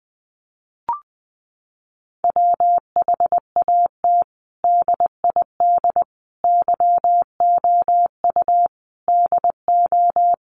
Morse code, heard as a keyed tone, about 19 wpm, 700 Hz. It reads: WHAT DID YOU DO
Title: Sentences from Top 100 Words 20wpm